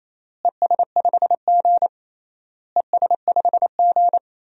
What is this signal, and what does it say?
Morse code, heard as a keyed tone, about 28 wpm, 700 Hz. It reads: ES5G ES5G
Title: DXCC-100 Call Signs - 4 Characters - Repeated 3x - 28wpm